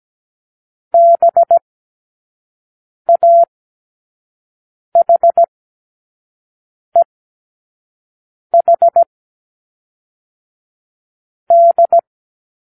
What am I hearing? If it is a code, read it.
Morse code, heard as a keyed tone, about 17 wpm, 680 Hz. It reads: B A H E H D
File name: k7qo / Lesson15(A-J)